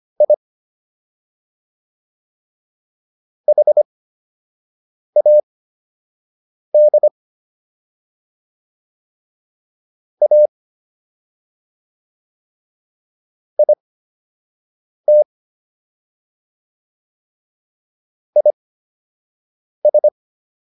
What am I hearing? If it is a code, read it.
Morse code, heard as a keyed tone, about 25 wpm, spaced as for 6 wpm, 600 Hz. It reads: I HAD A IT IS